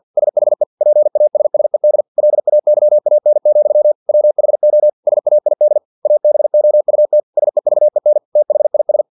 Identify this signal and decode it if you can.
Morse code, heard as a keyed tone, about 50 wpm, 600 Hz. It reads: SHE PASSED LAYAN, WHO SAID ABOUT SEVEN THIS
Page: Never 49wpm